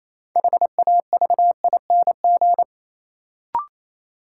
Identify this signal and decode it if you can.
Morse code, heard as a keyed tone, about 28 wpm, 700 Hz. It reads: HAVING
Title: Sets of 5 Words Encore - New Word Drill - 28wpm